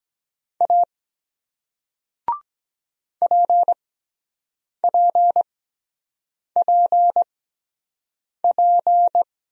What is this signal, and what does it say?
Morse code, heard as a keyed tone, about 23 wpm, spaced as for 7 wpm, 700 Hz. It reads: A PPPP